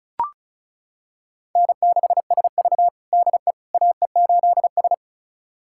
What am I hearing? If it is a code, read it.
Morse code, heard as a keyed tone, about 36 wpm, 700 Hz. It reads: N6SV DE AE8S
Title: USA Callsign DE Callsign - 4 Characters - 35wpm